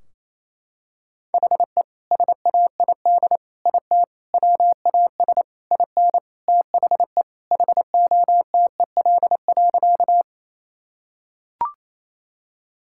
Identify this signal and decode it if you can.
Morse code, heard as a keyed tone, about 28 wpm, 700 Hz. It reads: HE SAID IT WAS IN THE HOTEL.
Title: Sentences from Top 1600 Words 28wpm